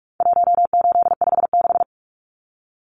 Morse code, heard as a keyed tone, about 44 wpm, 700 Hz. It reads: 1856
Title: Numbers - 4-Digit 45wpm